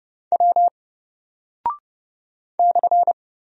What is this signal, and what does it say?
Morse code, heard as a keyed tone, about 30 wpm, 700 Hz. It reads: W /
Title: Single Character (Letter:Number:Punctuation:Prosign) 30wpm